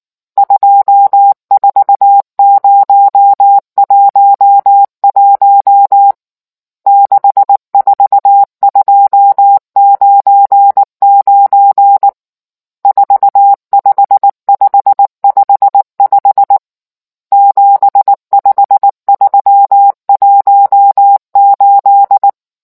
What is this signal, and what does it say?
Morse code, heard as a keyed tone, about 19 wpm, 800 Hz. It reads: 24011 64299 45555 75318